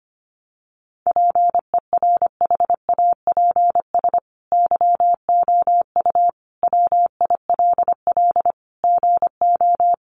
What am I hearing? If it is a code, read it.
Morse code, heard as a keyed tone, about 25 wpm, 700 Hz. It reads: PERHAPS YOU WILL GO
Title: Sentences from Top 500 Words 25wpm